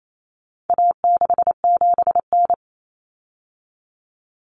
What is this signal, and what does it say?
Morse code, heard as a keyed tone, about 28 wpm, 700 Hz. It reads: A67N
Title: Common Cut Numbers - 4-Digit 28wpm